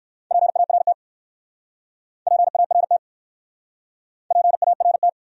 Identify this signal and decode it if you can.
Morse code, heard as a keyed tone, about 59 wpm, 700 Hz. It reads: PART PART PART